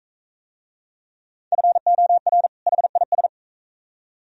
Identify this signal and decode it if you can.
Morse code, heard as a keyed tone, about 41 wpm, 700 Hz. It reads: FOR HIS